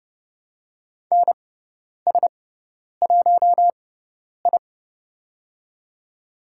Morse code, heard as a keyed tone, about 30 wpm, spaced as for 10 wpm, 700 Hz. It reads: NS1I
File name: Lesson 16g - Letter c - Call Signs 30wpm